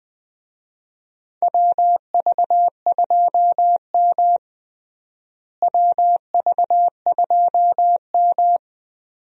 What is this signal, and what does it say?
Morse code, heard as a keyed tone, about 20 wpm, 700 Hz. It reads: WV2M WV2M
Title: USA Call Signs - 4 Characters - Repeated 2x - 20wpm